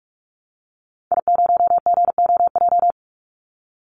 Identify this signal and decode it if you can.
Morse code, heard as a keyed tone, about 44 wpm, 700 Hz. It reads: I0GOJ
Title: Call Signs Repeated 2x - 45wpm